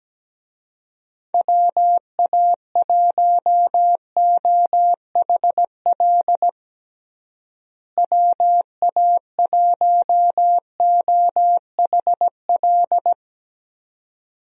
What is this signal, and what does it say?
Morse code, heard as a keyed tone, about 17 wpm, 700 Hz. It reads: WA1OHL WA1OHL
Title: USA Call Signs - 6 Characters - Repeated 2x - 17wpm